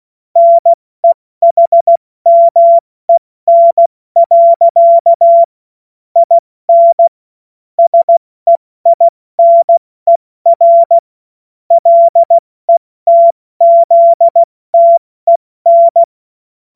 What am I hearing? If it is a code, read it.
Morse code, heard as a keyed tone, about 16 wpm, 680 Hz. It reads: NEHMEN. IN SEINER LETZTEN
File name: Morsekurs / Witze / Witz 06. September 2022 70 F80